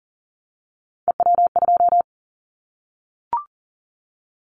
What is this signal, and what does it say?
Morse code, heard as a keyed tone, about 42 wpm, 700 Hz. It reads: EW2
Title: Callsign Prefixes for Top 50 DXCC - 40wpm